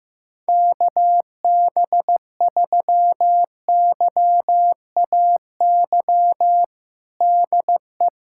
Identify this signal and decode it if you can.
Morse code, heard as a keyed tone, about 15 wpm, 700 Hz. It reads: KB3YAY DE
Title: USA Callsign DE Callsign - 6 Characters - 15wpm